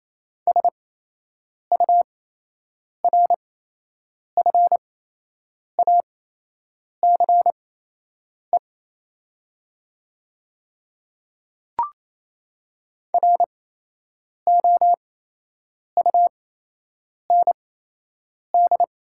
Morse code, heard as a keyed tone, about 27 wpm, spaced as for 8 wpm, 700 Hz. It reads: SURFACE ROUND
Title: 8x Character Spacing - Top 500 Words - 28wpm